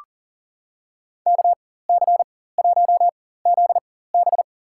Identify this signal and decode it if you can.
Morse code, heard as a keyed tone, about 37 wpm, spaced as for 18 wpm, 700 Hz. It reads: KC1ZB
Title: USA Call Signs - 5 Characters - 4x Character Spacing - 40wpm